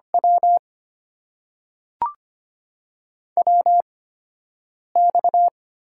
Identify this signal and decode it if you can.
Morse code, heard as a keyed tone, about 25 wpm, spaced as for 7 wpm, 700 Hz. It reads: W WX
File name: Top 90 QSO Elements - 8x Character Spacing - 25wpm